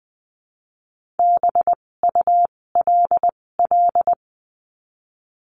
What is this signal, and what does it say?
Morse code, heard as a keyed tone, about 20 wpm, spaced as for 16 wpm, 700 Hz. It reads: BULL
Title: 256 Lesson 42a - Words 12-20wpm